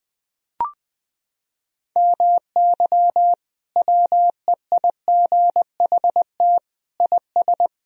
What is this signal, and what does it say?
Morse code, heard as a keyed tone, about 20 wpm, 700 Hz. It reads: MY WEIGHT IS
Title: Sentences from Top 1000 Words - Review Part 3 - 20wpm